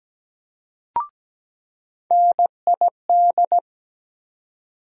Morse code, heard as a keyed tone, about 17 wpm, 700 Hz. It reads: NID
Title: USA Call Sign Postfixes - 3 Letters 17wpm